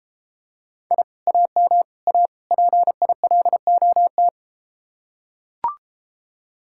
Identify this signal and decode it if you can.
Morse code, heard as a keyed tone, about 33 wpm, 700 Hz. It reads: I AM A PILOT